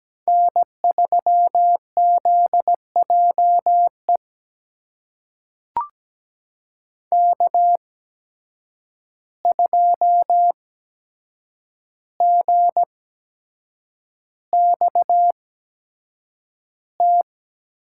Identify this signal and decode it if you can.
Morse code, heard as a keyed tone, about 17 wpm, spaced as for 7 wpm, 700 Hz. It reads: N3ZJE K2GXT
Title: USA Call Signs - 8x Character Spacing - 17wpm